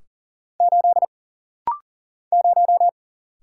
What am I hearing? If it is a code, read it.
Morse code, heard as a keyed tone, about 42 wpm, 700 Hz. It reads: 8 0
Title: Single Letter-Number - Rapid-Fire 40wpm